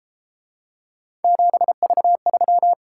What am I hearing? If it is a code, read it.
Morse code, heard as a keyed tone, about 33 wpm, 700 Hz. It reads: 7V3